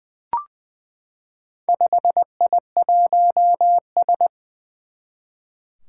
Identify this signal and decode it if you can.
Morse code, heard as a keyed tone, about 20 wpm, 700 Hz. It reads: 5I1S
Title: Four Challenging Letter-Number Combinations 20wpm